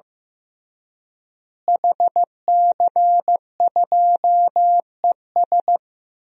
Morse code, heard as a keyed tone, about 16 wpm, 700 Hz. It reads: HC2ES
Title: DXCC-100 Call Signs - 5 Characters - Repeated 3x - 15wpm